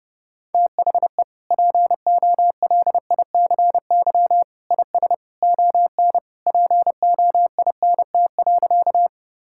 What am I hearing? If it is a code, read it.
Morse code, heard as a keyed tone, about 30 wpm, 700 Hz. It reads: THE POLICY IS ON POINT.
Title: Sentences from Top 1000 Words - Review Part 5 - 30wpm